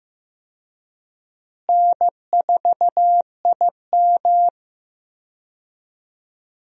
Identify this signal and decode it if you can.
Morse code, heard as a keyed tone, about 15 wpm, 700 Hz. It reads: N4IM
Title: USA Call Signs - 4 Characters - Repeated 2x - 15wpm